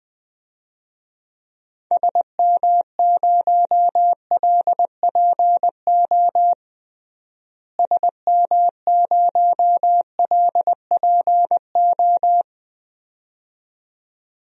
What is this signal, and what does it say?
Morse code, heard as a keyed tone, about 21 wpm, 700 Hz. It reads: SM0LPO SM0LPO
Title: DXCC-100 Call Signs - 6 Characters - Repeated 2x - 20wpm